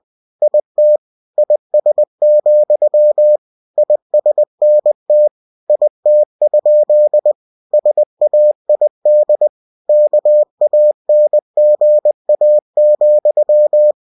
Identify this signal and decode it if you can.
Morse code, heard as a keyed tone, about 20 wpm, 600 Hz. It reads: IT IS, ISNT IT? SAID KANGA,
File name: cw-books / winnie-the-pooh / winnie-20wpm-page-0068